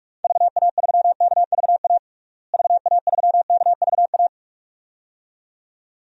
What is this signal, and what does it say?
Morse code, heard as a keyed tone, about 43 wpm, 700 Hz. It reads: VA3KVA VA3KVA
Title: DXCC-100 Call Signs - 6 Characters - Repeated 2x - 45wpm